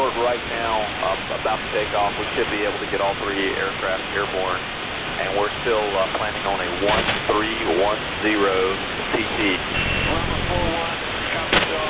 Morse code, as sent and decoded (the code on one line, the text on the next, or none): none